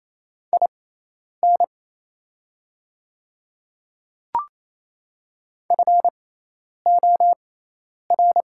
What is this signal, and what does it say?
Morse code, heard as a keyed tone, about 28 wpm, spaced as for 10 wpm, 700 Hz. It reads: IN FOR
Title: 6x Character Spacing - Top 200 Words - 28wpm